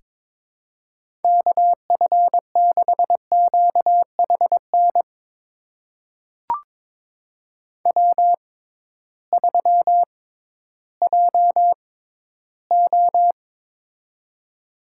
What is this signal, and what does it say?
Morse code, heard as a keyed tone, about 22 wpm, spaced as for 13 wpm, 700 Hz. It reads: KF6QHN W3JO
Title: USA Call Signs - 6x Character Spacing - 22wpm